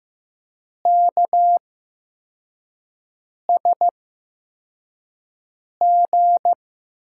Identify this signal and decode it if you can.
Morse code, heard as a keyed tone, about 15 wpm, spaced as for 4 wpm, 700 Hz. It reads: KSG
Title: USA Call Signs - 6 Characters - 8x Character Spacing - 15wpm